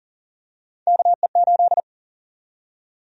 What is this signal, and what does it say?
Morse code, heard as a keyed tone, about 40 wpm, 700 Hz. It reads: KE8
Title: USA Call Sign Prefixes - Repeated 2x - 40wpm